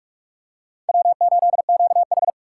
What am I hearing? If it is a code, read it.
Morse code, heard as a keyed tone, about 44 wpm, 700 Hz. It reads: W8QH
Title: USA Call Signs - 4 Characters - 8x Character Spacing - 45wpm